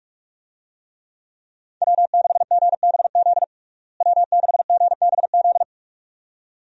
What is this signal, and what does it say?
Morse code, heard as a keyed tone, about 46 wpm, 700 Hz. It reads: W6GBZ W6GBZ
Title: USA Call Signs - 5 Characters - Repeated 2x - 45wpm